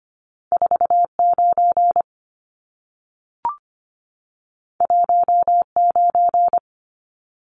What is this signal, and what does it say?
Morse code, heard as a keyed tone, about 25 wpm, 700 Hz. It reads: 49 19
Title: Numbers - 2-Digit 25wpm